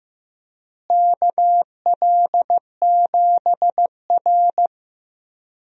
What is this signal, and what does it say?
Morse code, heard as a keyed tone, about 15 wpm, 700 Hz. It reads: KL7R
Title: DXCC-100 Call Signs - 4 Characters - Repeated 3x - 15wpm